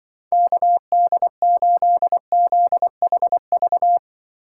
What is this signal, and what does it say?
Morse code, heard as a keyed tone, about 23 wpm, 700 Hz. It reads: KD8ZHV
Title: Speed Racing - USA 2x3 Call Signs 20wpm